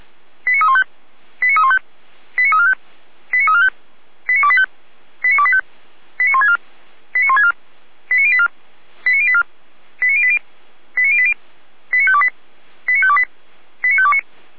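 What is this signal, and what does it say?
Five-tone selective call (ZVEI), heard as quick runs of five tones, tones of about 70 ms. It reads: 89315 89315 89345 89345 89285 89285 89164 89164 89084 89084 89080 89080 89528 89528 89529
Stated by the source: Received by SDR; ZVEI five-tone call.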